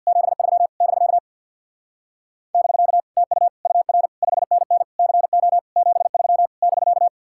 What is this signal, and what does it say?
Morse code, heard as a keyed tone, about 65 wpm, 700 Hz. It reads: 73 <BK> <BK> TU UR 5NN CO 73 <BK>